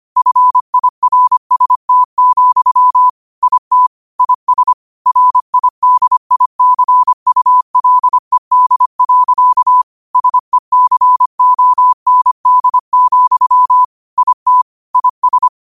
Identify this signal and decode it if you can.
Morse code, heard as a keyed tone, about 25 wpm, 1000 Hz. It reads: FIRST, IT IS RIDICULED. SECOND, IT IS